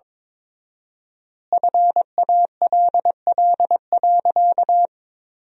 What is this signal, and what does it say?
Morse code, heard as a keyed tone, about 22 wpm, 700 Hz. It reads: FALL.
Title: 4x Word-Spacing - Sentences from Top 500 Words - 22wpm